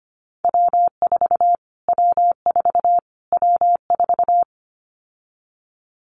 Morse code, heard as a keyed tone, about 25 wpm, 700 Hz. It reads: W4 W4 W4
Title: Callsign Prefixes for Top 25 DXCC - Repeated 3x - 25wpm